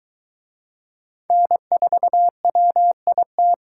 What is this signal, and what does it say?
Morse code, heard as a keyed tone, about 21 wpm, 700 Hz. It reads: N4WIT